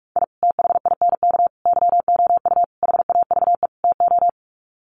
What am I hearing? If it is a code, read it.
Morse code, heard as a keyed tone, about 44 wpm, 700 Hz. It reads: I THINK YOU HAVE TO
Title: Sets of 5 Words 45wpm